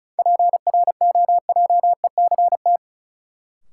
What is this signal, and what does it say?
Morse code, heard as a keyed tone, about 36 wpm, 700 Hz. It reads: PROJECT
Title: Sentences from Top 1400 Words - New Word Drill Alt - 35wpm